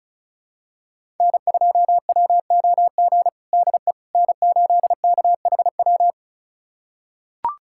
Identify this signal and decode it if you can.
Morse code, heard as a keyed tone, about 36 wpm, 700 Hz. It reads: N2WOG DE N8KHW
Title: USA Callsign DE Callsign - 5 Characters - 35wpm